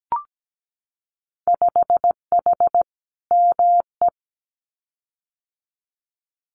Morse code, heard as a keyed tone, about 17 wpm, 700 Hz. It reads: 5H ME
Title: Winter Field Day Exchanges Encore 17wpm